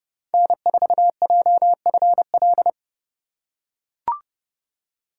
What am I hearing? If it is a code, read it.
Morse code, heard as a keyed tone, about 30 wpm, 700 Hz. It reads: N4JFL
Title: USA Call Signs - 5 Characters - 3x Character Spacing - 30wpm